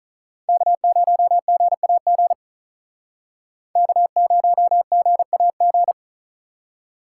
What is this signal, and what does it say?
Morse code, heard as a keyed tone, about 44 wpm, 700 Hz. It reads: K0GAG K0GAG